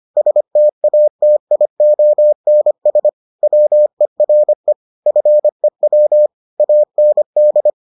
Morse code, heard as a keyed tone, about 25 wpm, 600 Hz. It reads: STATIONS WERE FEW AND